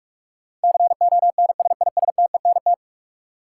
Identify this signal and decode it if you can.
Morse code, heard as a keyed tone, about 46 wpm, 700 Hz. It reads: CONSISTENT
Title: Sentences from Top 1500 Words - New Word Drill - 45wpm